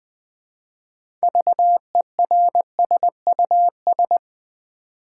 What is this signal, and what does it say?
Morse code, heard as a keyed tone, about 20 wpm, 700 Hz. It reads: VERSUS
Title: Build A Word - 2 Letters At A Time - 6-Letter Words - 20wpm